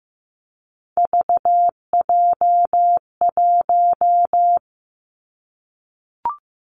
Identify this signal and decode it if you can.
Morse code, heard as a keyed tone, about 15 wpm, 700 Hz. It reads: VJ1
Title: Three Challenging Letter-Number Combinations 15wpm